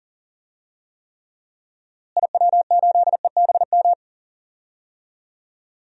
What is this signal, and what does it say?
Morse code, heard as a keyed tone, about 40 wpm, 700 Hz. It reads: IW8EBM
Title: DXCC-100 Call Signs - 6 Characters - Repeated 3x - 40wpm